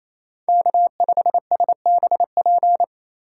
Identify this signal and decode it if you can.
Morse code, heard as a keyed tone, about 34 wpm, 700 Hz. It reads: K5SBP
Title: Speed Racing - USA Call Signs 28wpm